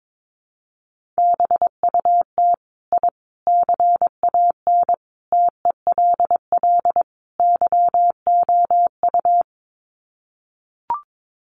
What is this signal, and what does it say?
Morse code, heard as a keyed tone, about 22 wpm, 700 Hz. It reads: BUT I CAN TELL YOU
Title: Sets of 5 Words Encore 22wpm